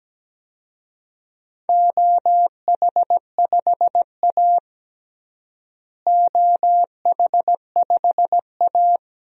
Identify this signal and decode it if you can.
Morse code, heard as a keyed tone, about 17 wpm, 700 Hz. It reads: OH5A OH5A
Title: DXCC-100 Call Signs - 4 Characters - Repeated 3x - 17wpm